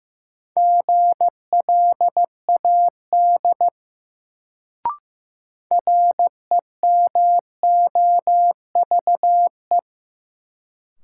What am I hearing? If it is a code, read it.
Morse code, heard as a keyed tone, about 15 wpm, 700 Hz. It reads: GLAD REMOVE
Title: Sentences from Top 800 Words - New Word Drill - 15wpm